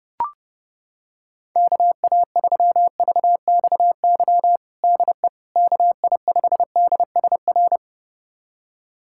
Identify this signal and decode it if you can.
Morse code, heard as a keyed tone, about 32 wpm, 700 Hz. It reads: KA3VXY DE KI5DSR
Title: USA Callsign DE Callsign 30wpm